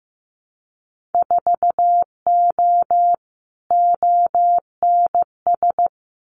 Morse code, heard as a keyed tone, about 15 wpm, 700 Hz. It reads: 4O ONS
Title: Winter Field Day Exchanges 15wpm